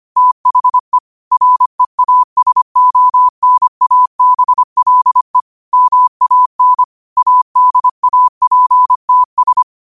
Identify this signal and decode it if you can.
Morse code, heard as a keyed tone, about 25 wpm, 1000 Hz. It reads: THE REASONABLE MAN ADAPTS
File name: qotdfiles / 25wpm